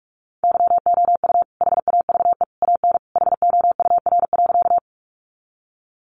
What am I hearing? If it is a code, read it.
Morse code, heard as a keyed tone, about 44 wpm, 700 Hz. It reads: YOU HAVE AN HOUR.
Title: Sentences from Top 400 Words 45wpm